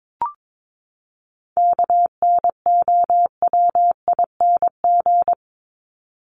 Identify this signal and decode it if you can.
Morse code, heard as a keyed tone, about 22 wpm, 700 Hz. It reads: KNOWING
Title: ING Suffix - Top 100 Words with Base Words 22wpm